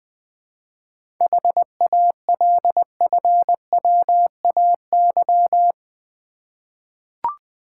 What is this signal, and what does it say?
Morse code, heard as a keyed tone, about 20 wpm, 700 Hz. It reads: HALFWAY
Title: 7 Letter Words - First 3 Letters Given - 20wpm